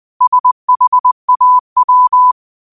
Morse code, heard as a keyed tone, about 20 wpm, 1000 Hz. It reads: SHAW
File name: qotdfiles / 20wpm